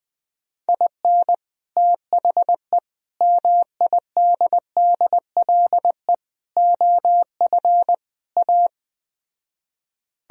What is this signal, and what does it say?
Morse code, heard as a keyed tone, about 20 wpm, 700 Hz. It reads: IN THE MIDDLE OF A